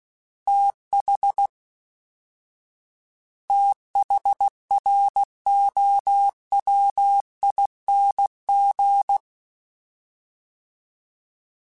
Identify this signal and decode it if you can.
Morse code, heard as a keyed tone, about 16 wpm, 800 Hz. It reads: TH THROWING
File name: TH / TH mp3s / TH Prefixes 1x1 @13